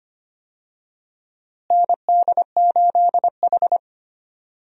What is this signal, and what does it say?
Morse code, heard as a keyed tone, about 25 wpm, 700 Hz. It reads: ND8H
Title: USA Call Signs - 4 Characters - Repeated 3x - 25wpm